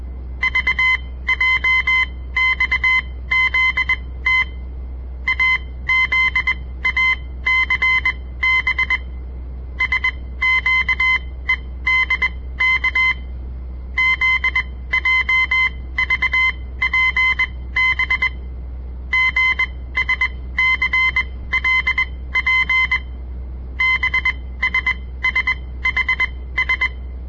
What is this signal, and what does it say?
Morse code, heard as a keyed tone, about 21 wpm, spaced as for 16 wpm, 2050 Hz. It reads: VJXZT AZACB SQEDK ZJVPB GSCLP BSSHS
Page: On the air Morse code practice from the Southern Montana Amatuer Radio Association; all code, no read back